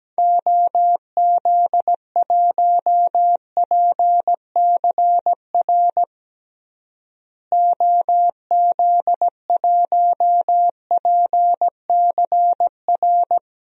Morse code, heard as a keyed tone, about 17 wpm, 700 Hz. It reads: OZ1PCR OZ1PCR
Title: DXCC-100 Call Signs - 6 Characters - Repeated 2x - 17wpm